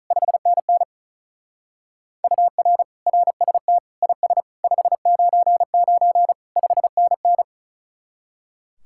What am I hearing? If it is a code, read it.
Morse code, heard as a keyed tone, about 33 wpm, 700 Hz. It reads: 5NN UR RST IS 599 5NN
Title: Speed-Racing Common QSO Phrases - 35wpm